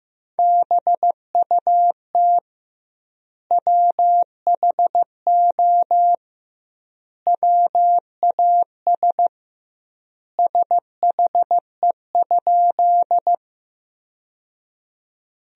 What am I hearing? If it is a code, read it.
Morse code, heard as a keyed tone, about 15 wpm, 700 Hz. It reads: BUT WHO WAS SHE?
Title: 2x Word-Spacing - Sentences from Top 100 Words - 15wpm